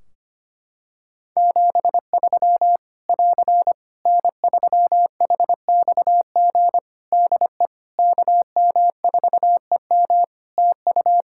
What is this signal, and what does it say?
Morse code, heard as a keyed tone, about 25 wpm, 700 Hz. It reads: 73 <AR> N3HXG DE KM4EM TU
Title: Practice QSO - Exchange 4 - Ending - 25wpm